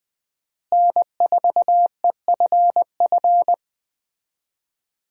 Morse code, heard as a keyed tone, about 20 wpm, 700 Hz. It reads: N4EFF
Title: USA Call Signs - 5 Characters - Repeated 3x - 20wpm